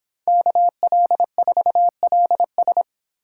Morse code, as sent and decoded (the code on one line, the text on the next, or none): -.- .-.. ....- .-.. ...
KL4LS